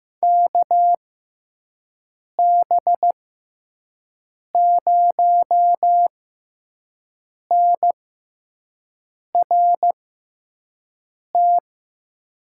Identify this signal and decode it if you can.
Morse code, heard as a keyed tone, about 15 wpm, spaced as for 5 wpm, 700 Hz. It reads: KB0NRT
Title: USA Call Signs - 6x Character Spacing - 15wpm